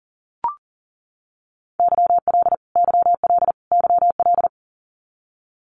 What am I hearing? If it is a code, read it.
Morse code, heard as a keyed tone, about 40 wpm, 700 Hz. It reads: YL YL YL
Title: Abbreviations Repeated 3x - 40wpm